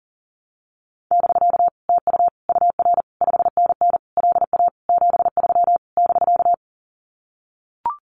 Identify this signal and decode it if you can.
Morse code, heard as a keyed tone, about 40 wpm, 700 Hz. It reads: <BK> TU UR 5NN LA 73 <BK>
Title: Common QSO Phrases Encore 40wpm